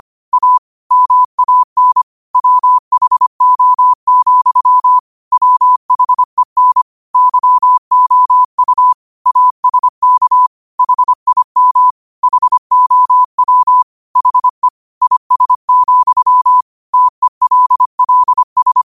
Morse code, heard as a keyed tone, about 25 wpm, 1000 Hz. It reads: A MAN WHO, WHEN YOU ASK HIM HOW HE IS, TELLS